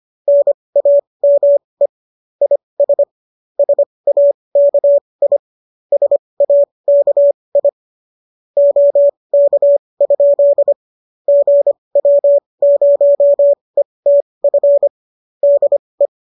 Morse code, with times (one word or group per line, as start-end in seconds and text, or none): 0.3-1.9 NAME
2.4-3.0 IS
3.6-5.4 SAKI
5.9-7.7 SAKI
8.6-10.7 OK?
11.3-14.9 GW0ETF
15.4-16.1 DE